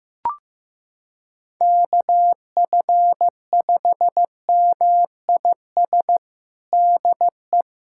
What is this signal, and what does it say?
Morse code, heard as a keyed tone, about 15 wpm, 700 Hz. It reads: KF5MIS DE
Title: USA Callsign DE Callsign - 6 Characters - 15wpm